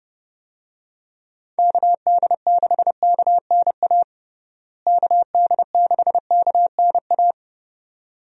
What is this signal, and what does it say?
Morse code, heard as a keyed tone, about 30 wpm, 700 Hz. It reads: KD6KNA KD6KNA
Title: USA Call Signs - 6 Characters - Repeated 2x - 30wpm